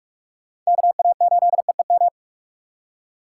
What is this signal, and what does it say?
Morse code, heard as a keyed tone, about 46 wpm, 700 Hz. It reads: KA8EEM